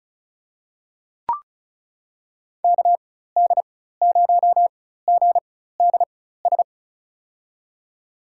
Morse code, heard as a keyed tone, about 34 wpm, spaced as for 16 wpm, 700 Hz. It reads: KD0GDS
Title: USA Call Signs - 6 Characters - 4x Character Spacing - 35wpm